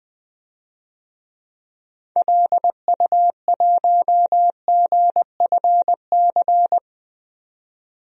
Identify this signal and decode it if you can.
Morse code, heard as a keyed tone, about 21 wpm, 700 Hz. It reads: LU1GFC
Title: DXCC-100 Call Signs - 6 Characters - 20wpm